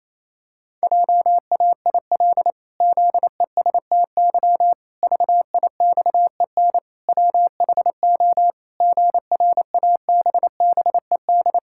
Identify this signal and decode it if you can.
Morse code, heard as a keyed tone, about 28 wpm, 700 Hz. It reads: JAIL ZESTY VIXEN WHO GRABBED
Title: Pangram Challenge v2 - 28wpm